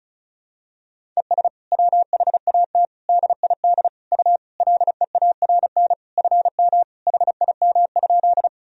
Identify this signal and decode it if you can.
Morse code, heard as a keyed tone, about 36 wpm, 700 Hz. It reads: ES WHAT DID U LEARN FM HIM?
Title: Sentences from Top 300 Words and Abbreviations 35wpm